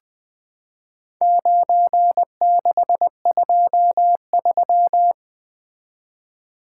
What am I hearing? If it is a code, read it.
Morse code, heard as a keyed tone, about 20 wpm, 700 Hz. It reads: 9623
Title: Four Letter-Number Combinations - 20wpm